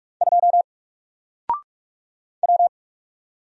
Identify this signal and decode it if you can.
Morse code, heard as a keyed tone, about 44 wpm, 700 Hz. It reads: 2 W
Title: Single Letter-Number Alt 45wpm